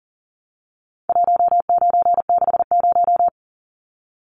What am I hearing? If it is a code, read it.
Morse code, heard as a keyed tone, about 40 wpm, 700 Hz. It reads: 1960
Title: Numbers - 4-Digit 40wpm